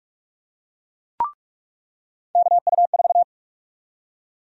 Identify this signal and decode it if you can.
Morse code, heard as a keyed tone, about 46 wpm, 700 Hz. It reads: KU4
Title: USA Call Sign Prefixes - 45wpm